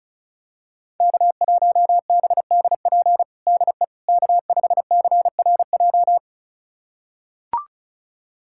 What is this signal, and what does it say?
Morse code, heard as a keyed tone, about 36 wpm, 700 Hz. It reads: K1BDP DE K5CRJ
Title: USA Callsign DE Callsign - 5 Characters - 35wpm